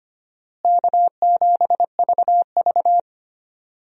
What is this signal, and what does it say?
Morse code, heard as a keyed tone, about 25 wpm, 700 Hz. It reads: K7VV
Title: USA Call Signs - 4 Characters - Repeated 3x - 25wpm